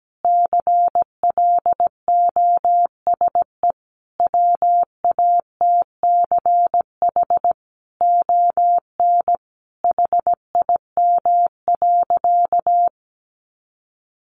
Morse code, heard as a keyed tone, about 17 wpm, 700 Hz. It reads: CLOSE WATCH ON HIM.
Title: Sentences from Top 1300 Words 17wpm